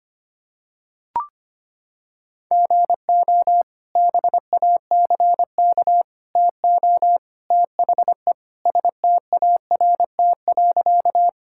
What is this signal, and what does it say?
Morse code, heard as a keyed tone, about 25 wpm, 700 Hz. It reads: GO BACK TO THE START.
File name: practice / 25wpm / Sentences with 3-Word Phrases - 25wpm